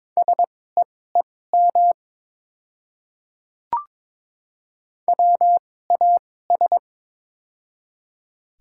Morse code, heard as a keyed tone, about 23 wpm, spaced as for 16 wpm, 700 Hz. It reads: SEEM WAS
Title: 2x Character Spacing - Top 300 Words - 22wpm